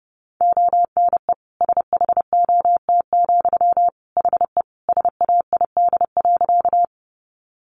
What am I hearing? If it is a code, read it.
Morse code, heard as a keyed tone, about 30 wpm, 700 Hz. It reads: ONE SHOT, HE SAID.
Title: Sentences from Top 1100 Words 30wpm